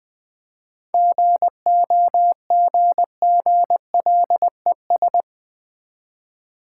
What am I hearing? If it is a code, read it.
Morse code, heard as a keyed tone, about 20 wpm, 700 Hz. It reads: GOGGLES
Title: Easy Words with Hints - 20wpm